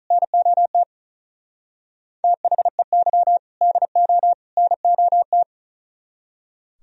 Morse code, heard as a keyed tone, about 44 wpm, 700 Hz. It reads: NOT THEY DO NOT